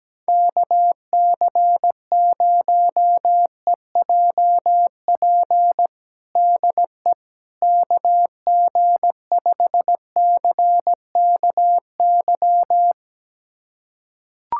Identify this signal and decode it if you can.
Morse code, heard as a keyed tone, about 17 wpm, 700 Hz. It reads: KC0EJP DE KG5CKY
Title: USA Callsign DE Callsign - 6 Characters - 17wpm